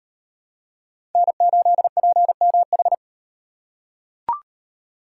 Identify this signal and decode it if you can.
Morse code, heard as a keyed tone, about 38 wpm, 700 Hz. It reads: N8PMH